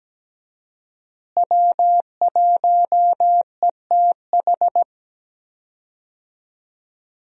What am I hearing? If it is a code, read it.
Morse code, heard as a keyed tone, about 17 wpm, 700 Hz. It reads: W1ETH
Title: USA Call Signs - 5 Characters - Repeated 2x - 17wpm